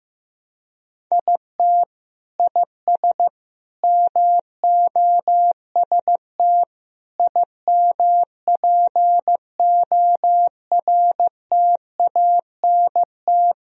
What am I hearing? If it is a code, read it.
Morse code, heard as a keyed tone, about 15 wpm, 700 Hz. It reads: IT IS MOST IMPORTANT
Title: Sentences from Top 1000 Words - Review Part 1 - 15wpm